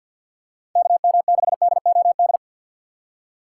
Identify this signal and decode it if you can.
Morse code, heard as a keyed tone, about 53 wpm, 700 Hz. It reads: KM6DOD
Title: USA Call Signs - 4x Character Spacing - 50wpm